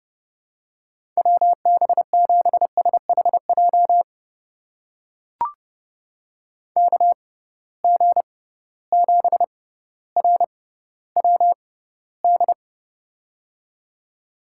Morse code, heard as a keyed tone, about 29 wpm, spaced as for 15 wpm, 700 Hz. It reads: WB7SHJ KG7RWD
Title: USA Call Signs - 6 Characters - 6x Character Spacing - 30wpm